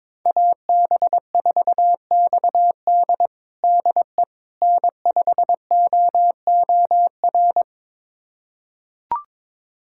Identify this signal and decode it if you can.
Morse code, heard as a keyed tone, about 22 wpm, 700 Hz. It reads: AB4XD DE N5OOR
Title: USA Callsign DE Callsign - 5 Characters - 22wpm